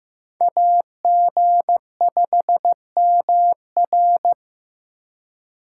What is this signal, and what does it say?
Morse code, heard as a keyed tone, about 18 wpm, 700 Hz. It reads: AG5MR
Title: Speed Racing - USA 2x2 Call Signs 15wpm